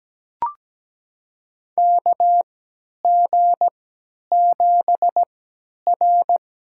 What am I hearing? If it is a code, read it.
Morse code, heard as a keyed tone, about 17 wpm, spaced as for 10 wpm, 700 Hz. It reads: KG7R
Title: USA Call Signs - 6 Characters - 3x Character Spacing - 17wpm